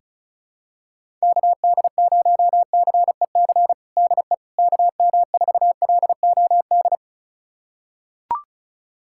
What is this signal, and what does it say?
Morse code, heard as a keyed tone, about 35 wpm, 700 Hz. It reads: KD0CEC DE KM4LOD
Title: USA Callsign DE Callsign - 6 Characters - 35wpm